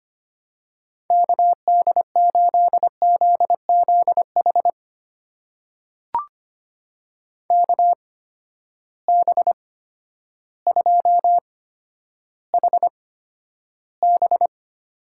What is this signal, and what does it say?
Morse code, heard as a keyed tone, about 25 wpm, spaced as for 11 wpm, 700 Hz. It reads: KD8ZZH KB2HB
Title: USA Call Signs - 6 Characters - 8x Character Spacing - 25wpm